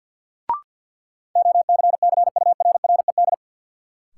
Morse code, heard as a keyed tone, about 52 wpm, 700 Hz. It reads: OCCURRED
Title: Sets of 5 Words Encore - New Word Drill - 50wpm